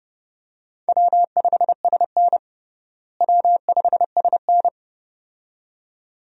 Morse code, heard as a keyed tone, about 32 wpm, 700 Hz. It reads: W5SN W5SN
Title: USA Call Signs - 4 Characters - Repeated 2x - 30wpm